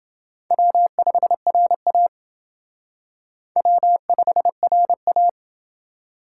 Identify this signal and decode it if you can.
Morse code, heard as a keyed tone, about 29 wpm, 700 Hz. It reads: W5RA W5RA